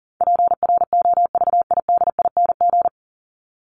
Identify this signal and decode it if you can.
Morse code, heard as a keyed tone, about 40 wpm, 700 Hz. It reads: PROVIDING
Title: ING Suffix - Top 100 Words with Base Words 40wpm